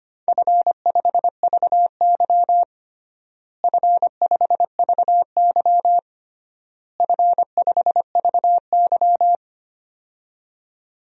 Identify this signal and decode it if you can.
Morse code, heard as a keyed tone, about 25 wpm, 700 Hz. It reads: F5VY F5VY F5VY
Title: DXCC-100 Call Signs - 4 Characters - Repeated 3x - 25wpm